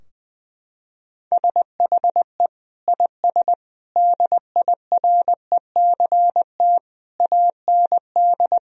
Morse code, heard as a keyed tone, about 20 wpm, 700 Hz. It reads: SHE IS DIRECT AND
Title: Sentences from Top 1000 Words - Review Part 4 - 20wpm